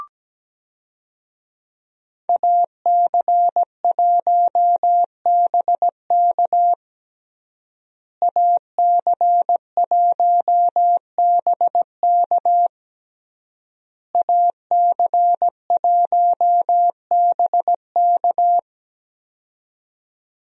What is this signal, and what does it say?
Morse code, heard as a keyed tone, about 17 wpm, 700 Hz. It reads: AC1BK AC1BK AC1BK
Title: USA Call Signs - 5 Characters - Repeated 3x - 17wpm